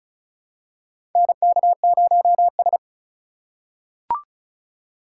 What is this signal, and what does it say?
Morse code, heard as a keyed tone, about 36 wpm, 700 Hz. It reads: NK0S
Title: USA Call Signs - 4 Characters - 2x Character Spacing - 35wpm